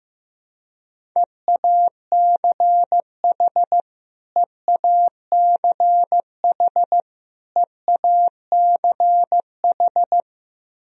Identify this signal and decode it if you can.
Morse code, heard as a keyed tone, about 16 wpm, 700 Hz. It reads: EACH EACH EACH
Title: Top 100 Words Repeated 15wpm